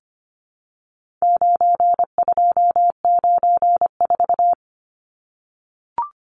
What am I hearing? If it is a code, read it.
Morse code, heard as a keyed tone, about 25 wpm, 700 Hz. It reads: 9294
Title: Numbers - 4-Digit 25wpm